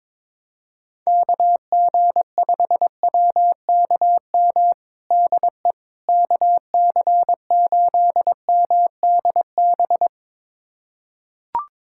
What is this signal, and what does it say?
Morse code, heard as a keyed tone, about 22 wpm, 700 Hz. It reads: KG5WKM DE KC8MDB
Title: USA Callsign DE Callsign 22wpm